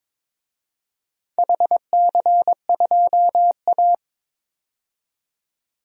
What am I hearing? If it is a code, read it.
Morse code, heard as a keyed tone, about 21 wpm, 700 Hz. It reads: HC2A
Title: DXCC-100 Call Signs - 4 Characters - 22wpm